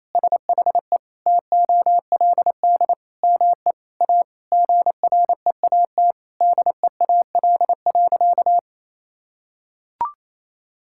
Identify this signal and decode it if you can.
Morse code, heard as a keyed tone, about 28 wpm, 700 Hz. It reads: SHE TOLD ME A GREAT DEAL.
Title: Sentences from Top 1000 Words - Review Part 6 - 28wpm